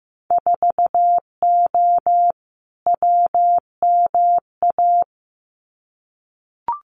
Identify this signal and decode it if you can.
Morse code, heard as a keyed tone, about 15 wpm, 700 Hz. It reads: 4O WMA
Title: Winter Field Day Exchanges Encore 15wpm